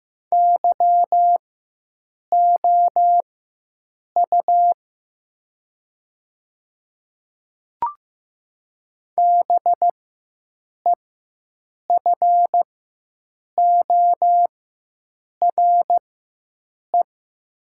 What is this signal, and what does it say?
Morse code, heard as a keyed tone, about 15 wpm, spaced as for 7 wpm, 700 Hz. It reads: YOU BEFORE
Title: 4x Character Spacing - Top 200 Words - 15wpm